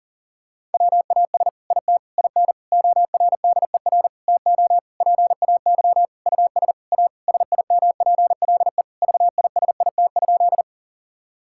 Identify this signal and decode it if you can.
Morse code, heard as a keyed tone, about 37 wpm, 700 Hz. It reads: WAS IT IN ORDER TO PAY US A SIMPLE VISIT?
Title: Sentences from Top 1000 Words - Review Part 6 - 40wpm